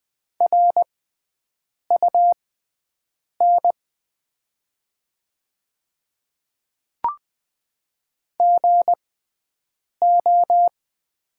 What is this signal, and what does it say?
Morse code, heard as a keyed tone, about 20 wpm, spaced as for 7 wpm, 700 Hz. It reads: RUN GO
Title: 6x Character Spacing - Top 300 Words - 20wpm